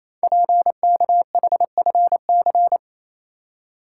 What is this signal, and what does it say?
Morse code, heard as a keyed tone, about 28 wpm, 690 Hz. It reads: PKHFC
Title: Code Groups - 5 Letters - 28wpm